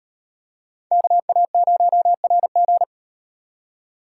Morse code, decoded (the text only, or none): KA0RG